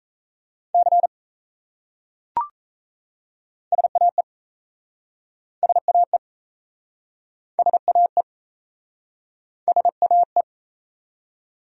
Morse code, heard as a keyed tone, about 38 wpm, 700 Hz. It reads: C SAE SAE SAE SAE